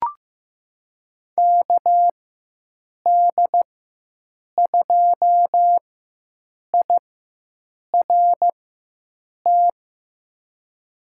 Morse code, heard as a keyed tone, about 15 wpm, spaced as for 7 wpm, 700 Hz. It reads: KD2IRT
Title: USA Call Signs - 4x Character Spacing - 15wpm